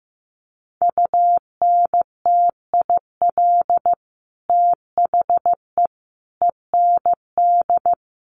Morse code, heard as a keyed tone, about 15 wpm, 700 Hz. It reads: UNTIL THE END
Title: Sets of 5 Words Encore 15wpm